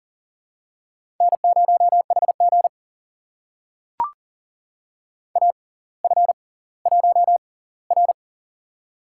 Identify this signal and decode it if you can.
Morse code, heard as a keyed tone, about 41 wpm, spaced as for 21 wpm, 690 Hz. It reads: N0HG AF1R
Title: USA Call Signs - 4 Characters - 6x Character Spacing - 40wpm